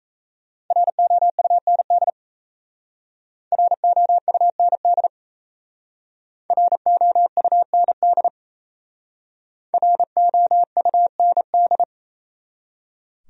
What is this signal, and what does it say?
Morse code, heard as a keyed tone, about 34 wpm, 700 Hz. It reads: ROUND ROUND ROUND ROUND
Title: Speed-Racing Top 200 Words 28wpm